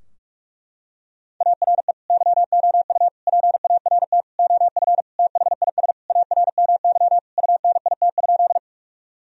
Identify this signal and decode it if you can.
Morse code, heard as a keyed tone, about 46 wpm, 700 Hz. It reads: ARE YOU PART OF THIS ARMY UNIT?
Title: Sentences from Top 1000 Words - Review Part 2 - 45wpm